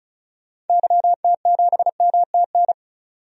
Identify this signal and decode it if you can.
Morse code, heard as a keyed tone, about 36 wpm, 700 Hz. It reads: YT7MTN
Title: DXCC-100 Call Signs - 6 Characters - 35wpm